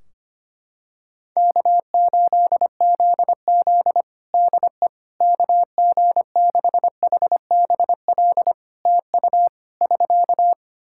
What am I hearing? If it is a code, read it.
Morse code, heard as a keyed tone, about 25 wpm, 700 Hz. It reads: K8ZZ DE KG6HBL TU <SK>
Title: Practice QSO - Exchange 4 - Ending - 25wpm